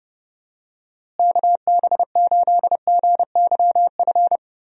Morse code, heard as a keyed tone, about 28 wpm, 700 Hz. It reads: KB8GYF
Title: USA Call Signs - 6 Characters - Repeated 3x - 30wpm